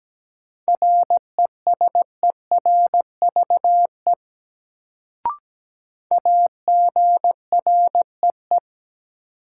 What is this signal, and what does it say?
Morse code, heard as a keyed tone, about 17 wpm, 700 Hz. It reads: RESERVE AGREE
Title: Sentences from Top 1000 Words - Review Part 6 - New Word Drill - 17wpm